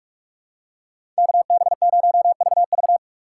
Morse code, heard as a keyed tone, about 46 wpm, 700 Hz. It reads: KB0VV
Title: USA Call Signs - 5 Characters - 6x Character Spacing - 45wpm